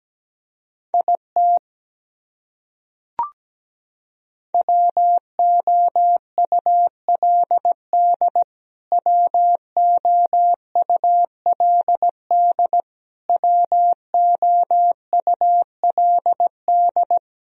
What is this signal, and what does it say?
Morse code, heard as a keyed tone, about 17 wpm, 700 Hz. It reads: IT WOULD WOULD WOULD
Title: Top 100 Words Repeated 17wpm